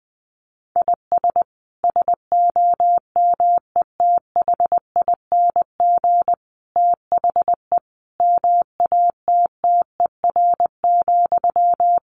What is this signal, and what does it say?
Morse code, heard as a keyed tone, about 19 wpm, 700 Hz. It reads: IS SOMETHING THE MATTER,
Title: Sentences from Top 700 Words 20wpm